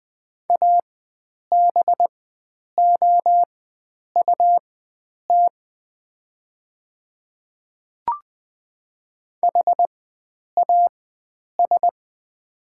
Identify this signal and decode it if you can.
Morse code, heard as a keyed tone, about 20 wpm, spaced as for 9 wpm, 700 Hz. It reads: ABOUT HAS
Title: 4x Character Spacing - Top 100 Words - 20wpm